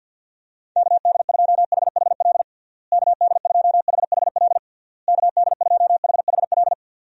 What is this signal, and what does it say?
Morse code, heard as a keyed tone, about 52 wpm, 700 Hz. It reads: KD2HHL KD2HHL KD2HHL
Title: USA Call Signs - 6 Characters - Repeated 3x - 50wpm